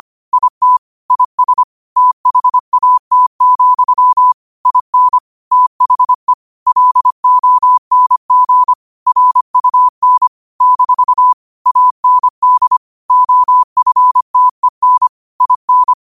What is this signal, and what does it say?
Morse code, heard as a keyed tone, about 25 wpm, 1000 Hz. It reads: IT IS THAT, IN THE LONG RUN - AND OFTEN IN
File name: qotdfiles / 25wpm